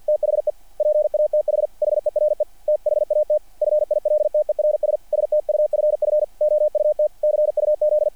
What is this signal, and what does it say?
Morse code, heard as a keyed tone, about 48 wpm, 660 Hz. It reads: THE PATH HERE THAT FILTERS STUFF OUT CUZ
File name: archived / CRSnet